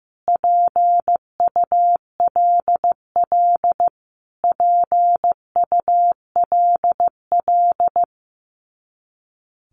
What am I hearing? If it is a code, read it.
Morse code, heard as a keyed tone, about 15 wpm, 700 Hz. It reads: PULL PULL
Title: Top 500 Words Repeated 15wpm